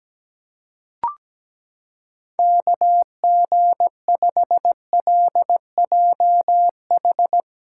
Morse code, heard as a keyed tone, about 17 wpm, 700 Hz. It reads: KG5LJH
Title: USA Callsign DE Callsign 17wpm